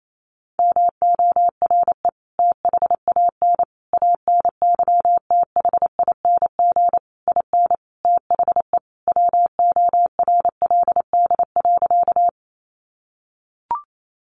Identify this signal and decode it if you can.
Morse code, heard as a keyed tone, about 28 wpm, 700 Hz. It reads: MORE THAN ANYTHING IN THE WORLD.
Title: Sentences from Top 1000 Words 28wpm